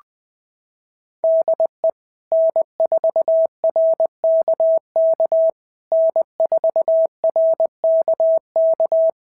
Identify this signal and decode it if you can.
Morse code, heard as a keyed tone, about 20 wpm, 660 Hz. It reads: DE N4RKK N4RKK
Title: POTA QSO Practice - 20wpm